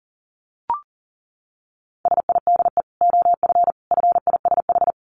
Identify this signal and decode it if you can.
Morse code, heard as a keyed tone, about 40 wpm, 700 Hz. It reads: SIDE OF FISH
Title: Dit Challenge 40wpm